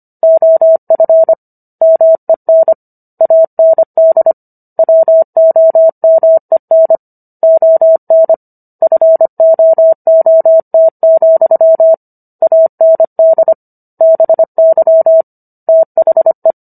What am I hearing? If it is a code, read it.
Morse code, heard as a keyed tone, about 25 wpm, 650 Hz. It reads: OF MEN AND WOMEN ON FOOT, AND BY THE